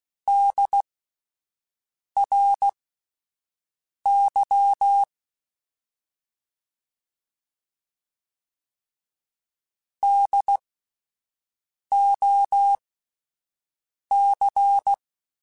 Morse code, heard as a keyed tone, about 16 wpm, spaced as for 5 wpm, 800 Hz. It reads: DRY DOC
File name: Spaced Alphas / Spaced Alphas mp3s / Ds sp @10